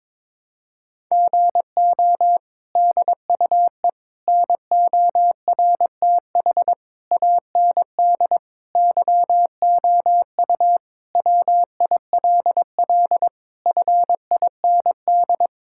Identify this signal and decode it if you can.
Morse code, heard as a keyed tone, about 22 wpm, 700 Hz. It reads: GO DUE NORTH AND YOU WILL FIND
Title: Sentences from Top 1000 Words - Review Part 6 - 22wpm